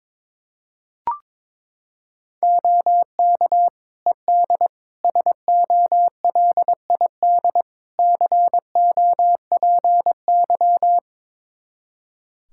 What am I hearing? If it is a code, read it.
Morse code, heard as a keyed tone, about 22 wpm, 700 Hz. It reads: OK ED SOLID COPY
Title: K1USN SST QSO Practice 22wpm